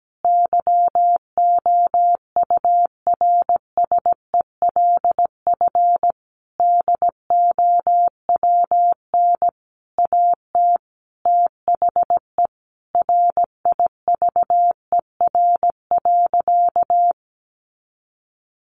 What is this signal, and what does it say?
Morse code, heard as a keyed tone, about 17 wpm, 700 Hz. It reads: YOURSELF DOWN AT THE RIVER.
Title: Sentences from Top 1200 Words 17wpm